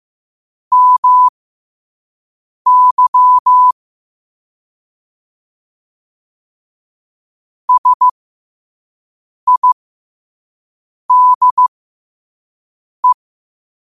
Morse code, heard as a keyed tone, about 15 wpm, spaced as for 5 wpm, 1000 Hz. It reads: MY SIDE
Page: Quotes for Wed, 13 Aug 2025 in Morse Code at 5 words per minute.